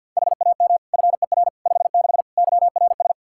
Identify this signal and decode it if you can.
Morse code, heard as a keyed tone, about 52 wpm, 700 Hz. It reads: HAM FER 56 YRS
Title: Practice QSO - Exchange 3 - More Info - 50wpm